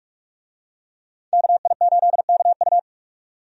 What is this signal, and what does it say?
Morse code, heard as a keyed tone, about 46 wpm, 700 Hz. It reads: KI8KU
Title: USA Call Signs - 5 Characters - 45wpm